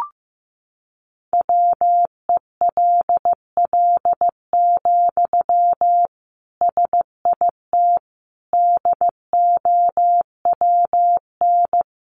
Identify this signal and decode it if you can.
Morse code, heard as a keyed tone, about 14 wpm, 700 Hz. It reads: WELL, SIT DOWN
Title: Sentences from Top 500 Words 15wpm